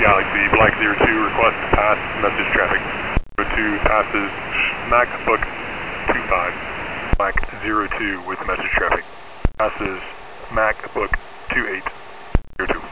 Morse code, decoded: none